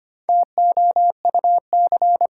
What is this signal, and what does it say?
Morse code, heard as a keyed tone, about 25 wpm, 700 Hz. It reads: TOUC
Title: Sentences from Top 1200 Words - New Word Drill Alt - 25wpm